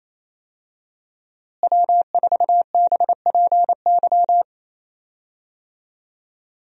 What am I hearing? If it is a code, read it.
Morse code, heard as a keyed tone, about 28 wpm, 700 Hz. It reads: W4BPY
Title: USA Call Signs - 5 Characters - Repeated 3x - 28wpm